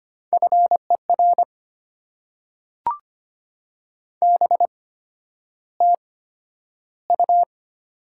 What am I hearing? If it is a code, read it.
Morse code, heard as a keyed tone, about 25 wpm, spaced as for 11 wpm, 700 Hz. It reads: FER BTU
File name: Top 90 QSO Elements - 8x Character Spacing - 25wpm